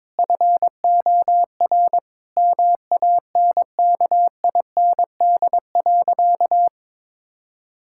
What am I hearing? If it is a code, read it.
Morse code, heard as a keyed tone, about 22 wpm, 700 Hz. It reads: FOR MANKIND.
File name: Sentences with 9-Word Phrases - 22wpm